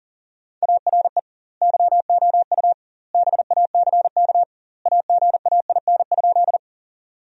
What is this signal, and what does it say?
Morse code, heard as a keyed tone, about 40 wpm, 700 Hz. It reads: ARE YOU BACK AGAIN?
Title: 2x Word-Spacing - Sentences from Top 200 Words - 40wpm